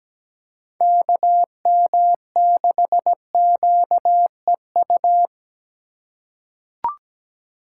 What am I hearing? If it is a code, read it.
Morse code, heard as a keyed tone, about 17 wpm, 700 Hz. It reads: KM6QEU
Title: USA Call Signs - 3x Character Spacing - 17wpm